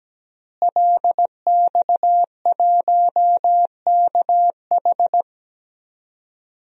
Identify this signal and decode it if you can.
Morse code, heard as a keyed tone, about 17 wpm, 700 Hz. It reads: LX1KH
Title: DXCC-100 Call Signs - 5 Characters - Repeated 3x - 17wpm